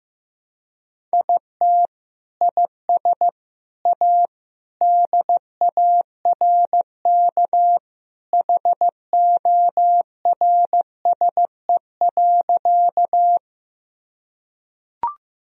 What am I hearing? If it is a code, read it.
Morse code, heard as a keyed tone, about 15 wpm, 700 Hz. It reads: IT IS A DARK HORSE.
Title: Sentences from Top 1000 Words - Review Part 1 - 15wpm